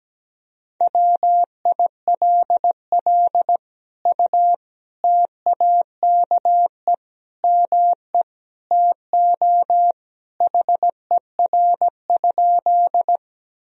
Morse code, heard as a keyed tone, about 17 wpm, 700 Hz. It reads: WILL U TAKE ME TO HER?
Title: Sentences from Top 200 Words and Abbreviations 17wpm